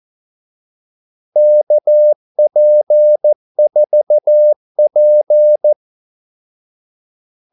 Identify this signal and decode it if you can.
Morse code, heard as a keyed tone, about 14 wpm, 600 Hz. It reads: KP4P
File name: Callsigns-14wpm